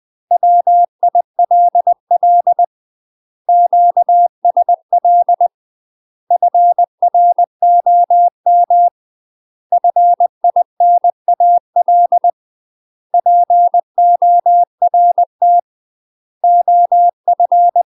Morse code, heard as a keyed tone, about 20 wpm, 700 Hz. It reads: WILL QSL FROM FINAL PORT OF
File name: UK MASTER QSOs TRAINING/20WPM